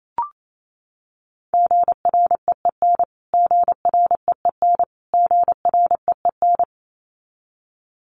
Morse code, heard as a keyed tone, about 28 wpm, 700 Hz. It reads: GREEN GREEN GREEN
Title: Top 500 Words Repeated 28wpm